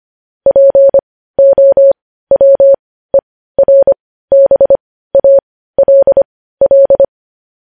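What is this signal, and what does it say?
Morse code, heard as a keyed tone, about 25 wpm, spaced as for 15 wpm, 550 Hz. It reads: POWERBALL
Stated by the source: Morse code news headlines practice at 15, preceded by vvv.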